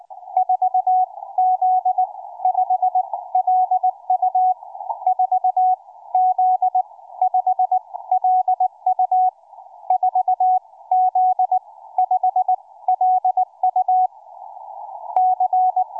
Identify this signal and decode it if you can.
Morse code, heard as a keyed tone, about 20 wpm, spaced as for 15 wpm, 740 Hz. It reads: 4Z5LU 4Z5LU 4Z5LU C
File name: cw forum / 4z5lu 20m 19h25